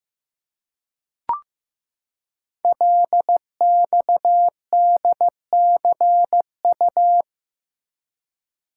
Morse code, heard as a keyed tone, about 15 wpm, 700 Hz. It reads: LXDCU
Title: Code Groups - 5 Letters - 15wpm